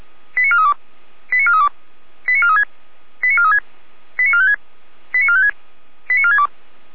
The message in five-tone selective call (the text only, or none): 89432 89432 89436 89436 89456 89456 89462